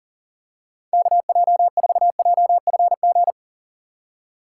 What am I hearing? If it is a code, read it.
Morse code, heard as a keyed tone, about 37 wpm, 700 Hz. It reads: KJ4JFG
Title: USA Call Signs - 6 Characters - 4x Character Spacing - 40wpm